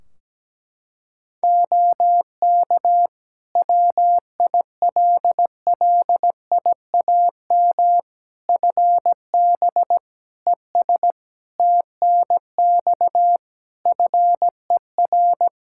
Morse code, heard as a keyed tone, about 17 wpm, 700 Hz. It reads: OK WILLIAM FB ES TNX FER
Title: Practice QSO - Exchange 2 - Info - 17wpm